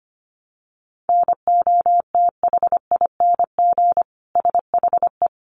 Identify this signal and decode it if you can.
Morse code, heard as a keyed tone, about 25 wpm, 700 Hz. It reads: NOTHING SHE
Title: Sentences from Top 500 Words 25wpm